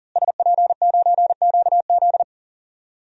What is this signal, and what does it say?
Morse code, heard as a keyed tone, about 40 wpm, 700 Hz. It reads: SP9QZ
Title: DXCC-100 Call Signs - 5 Characters - Repeated 3x - 40wpm